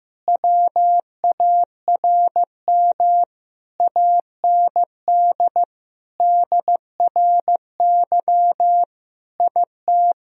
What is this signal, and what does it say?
Morse code, heard as a keyed tone, about 15 wpm, 700 Hz. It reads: WARM AND DRY IT
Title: Sentences from Top 1000 Words - Review Part 3 - 15wpm